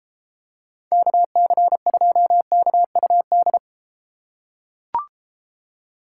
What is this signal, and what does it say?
Morse code, heard as a keyed tone, about 32 wpm, 700 Hz. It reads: KC2KUD